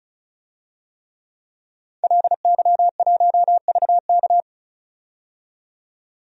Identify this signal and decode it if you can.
Morse code, heard as a keyed tone, about 36 wpm, 700 Hz. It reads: LY1VK
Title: DXCC-100 Call Signs - 5 Characters - 35wpm